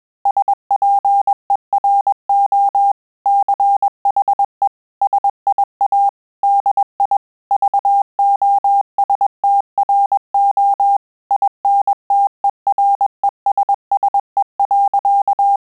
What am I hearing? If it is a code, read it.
Morse code, heard as a keyed tone, about 21 wpm, 800 Hz. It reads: SPERO CHE SIA DI VOSTRO INTERESSE.
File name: Telegraphy-group / breve testo midi